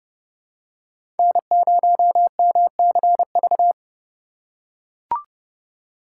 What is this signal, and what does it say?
Morse code, heard as a keyed tone, about 30 wpm, 700 Hz. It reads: N0MCV
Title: USA Call Signs - 5 Characters - 2x Character Spacing - 30wpm